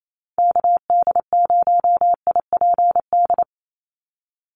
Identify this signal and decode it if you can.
Morse code, heard as a keyed tone, about 28 wpm, 700 Hz. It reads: KD0IPD
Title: USA Call Signs 28wpm